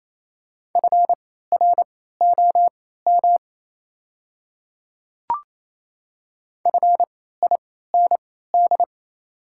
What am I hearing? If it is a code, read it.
Morse code, heard as a keyed tone, about 28 wpm, spaced as for 16 wpm, 690 Hz. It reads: FROM FIND
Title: 3x Character Spacing - Top 100 Words - 28wpm